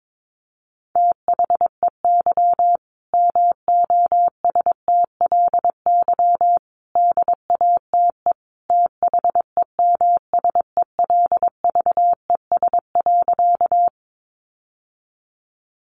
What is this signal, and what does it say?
Morse code, heard as a keyed tone, about 22 wpm, 700 Hz. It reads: THEY MOSTLY DATE THEMSELVES.
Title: Sentences from Top 1300 Words 22wpm